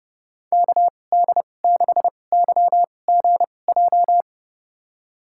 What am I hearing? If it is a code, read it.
Morse code, heard as a keyed tone, about 30 wpm, spaced as for 22 wpm, 700 Hz. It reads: KD6YGJ
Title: USA Call Signs - 6 Characters - 2x Character Spacing - 30wpm